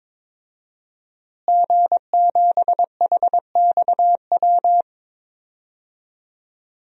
Morse code, heard as a keyed tone, about 22 wpm, 700 Hz. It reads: G7HXW
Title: DXCC-100 Call Signs - 5 Characters - Repeated 2x - 22wpm